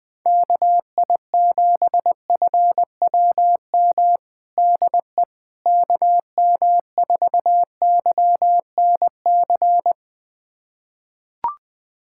Morse code, heard as a keyed tone, about 20 wpm, 700 Hz. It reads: KI7FWM DE KM4YNC
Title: USA Callsign DE Callsign - 6 Characters - 20wpm